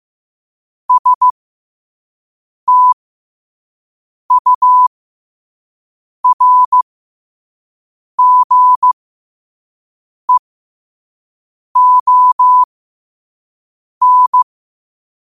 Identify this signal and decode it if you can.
Morse code, heard as a keyed tone, about 15 wpm, spaced as for 5 wpm, 1000 Hz. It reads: STURGEON